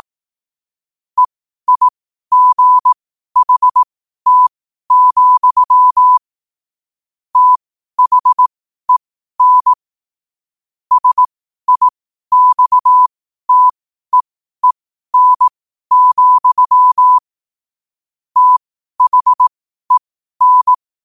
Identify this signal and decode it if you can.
Morse code, heard as a keyed tone, about 18 wpm, spaced as for 12 wpm, 1000 Hz. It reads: EIGHT, THEN SIXTEEN, THEN
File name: qotdfiles / 12wpm